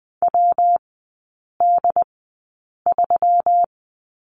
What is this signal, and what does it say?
Morse code, heard as a keyed tone, about 20 wpm, spaced as for 8 wpm, 700 Hz. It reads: WD3
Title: 133 Lesson 22e - Question Mark - Three Characters 20wpm